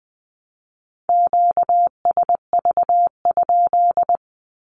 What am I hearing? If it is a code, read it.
Morse code, heard as a keyed tone, about 20 wpm, 700 Hz. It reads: QSV?
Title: Q-Codes Repeated 3x 20wpm